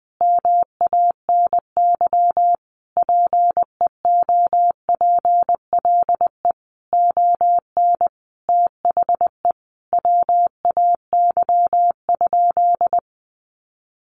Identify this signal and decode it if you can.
Morse code, heard as a keyed tone, about 20 wpm, 700 Hz. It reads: MANY PEOPLE ON THE WAY?
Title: Sentences from Top 1100 Words 20wpm